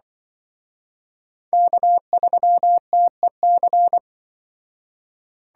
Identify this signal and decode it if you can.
Morse code, heard as a keyed tone, about 25 wpm, 700 Hz. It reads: K3TEC
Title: Speed Racing - USA 1x3 Call Signs 20wpm